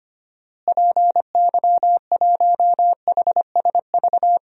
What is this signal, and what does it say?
Morse code, heard as a keyed tone, about 25 wpm, 700 Hz. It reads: PY1HSV
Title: DXCC-100 Call Signs - 6 Characters - Repeated 2x - 25wpm